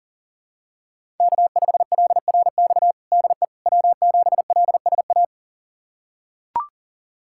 Morse code, heard as a keyed tone, about 37 wpm, 700 Hz. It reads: K5LRX DE W7LSA
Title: USA Callsign DE Callsign 40wpm